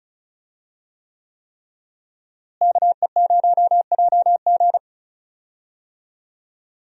Morse code, heard as a keyed tone, about 34 wpm, 700 Hz. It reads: KE0JG
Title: USA Call Signs - 5 Characters - Repeated 3x - 35wpm